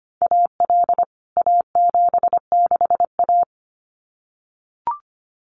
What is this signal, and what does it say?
Morse code, heard as a keyed tone, about 25 wpm, 700 Hz. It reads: AL A76A
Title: CWT Exchange 25wpm